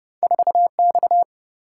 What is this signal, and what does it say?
Morse code, heard as a keyed tone, about 30 wpm, 700 Hz. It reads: 4X
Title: Single-Letter Single-Number 30wpm